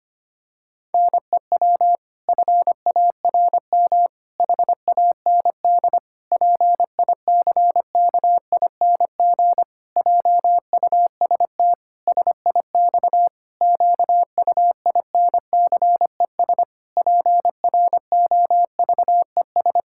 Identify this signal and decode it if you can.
Morse code, heard as a keyed tone, about 25 wpm, 700 Hz. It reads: NEW FARM HAND PICKING JUST SIX QUINCES PROVES
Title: Pangram Challenge v2 - 25wpm